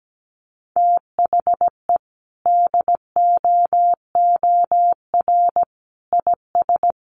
Morse code, heard as a keyed tone, about 17 wpm, 700 Hz. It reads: THE DOOR IS
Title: Sentences from Top 1400 Words 17wpm